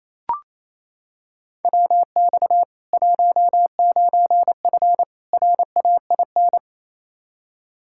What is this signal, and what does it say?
Morse code, heard as a keyed tone, about 28 wpm, 700 Hz. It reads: WX 19F RAIN
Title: K1USN SST QSO Practice 28wpm